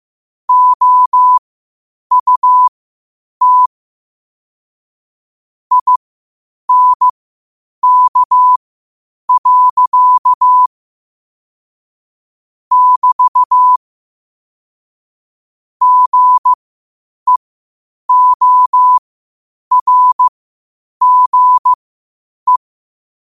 Morse code, heard as a keyed tone, about 15 wpm, spaced as for 8 wpm, 1000 Hz. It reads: OUT INK. <BT> GEORGE
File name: qotdfiles / 8wpm